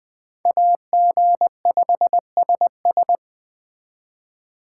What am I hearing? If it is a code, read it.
Morse code, heard as a keyed tone, about 20 wpm, 700 Hz. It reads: AG5SS